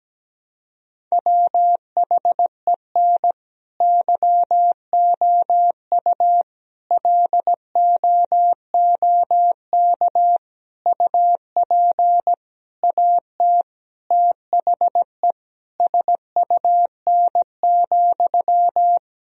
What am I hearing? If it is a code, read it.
Morse code, heard as a keyed tone, about 17 wpm, 700 Hz. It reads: WHEN YOU LOOK UP AT THE SUN,
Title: Sentences from Top 1000 Words - Review Part 1 - 17wpm